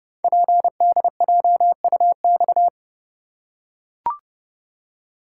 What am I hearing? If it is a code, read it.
Morse code, heard as a keyed tone, about 30 wpm, 700 Hz. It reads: PDJUX
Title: Code Groups - 5 Letters - 30wpm